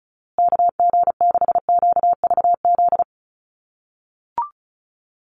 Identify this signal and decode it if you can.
Morse code, heard as a keyed tone, about 34 wpm, 700 Hz. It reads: KG6QVZ
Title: USA Call Signs Repeated 3x 35wpm